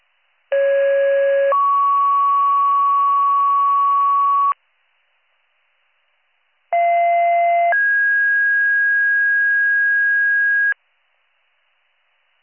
Two-tone page, heard as a one-second tone, then a three-second tone, about 550 then 1120 Hz, about 690 then 1650 Hz. It is heard from 0.5 to 10.7 s.